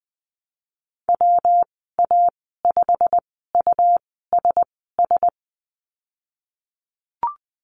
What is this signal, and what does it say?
Morse code, heard as a keyed tone, about 20 wpm, spaced as for 14 wpm, 700 Hz. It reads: WA5USS
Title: 140 Lesson 23g - Letter f - Call Signs 20wpm